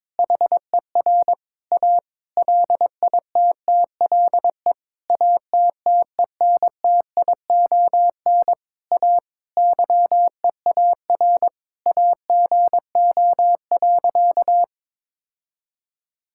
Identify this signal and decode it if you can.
Morse code, heard as a keyed tone, about 22 wpm, 700 Hz. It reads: HER A LITTLE ATTENTION A YEAR AGO.
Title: Sentences from Top 1000 Words - Review Part 6 - 22wpm